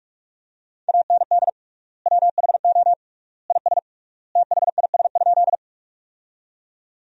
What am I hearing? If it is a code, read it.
Morse code, heard as a keyed tone, about 42 wpm, 700 Hz. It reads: AND WHO IS THIS?
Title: 3x Word-Spacing - Sentences from Top 100 Words - 45wpm